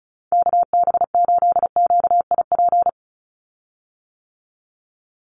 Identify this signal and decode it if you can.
Morse code, heard as a keyed tone, about 34 wpm, 700 Hz. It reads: KB8QIP
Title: USA Call Signs 35wpm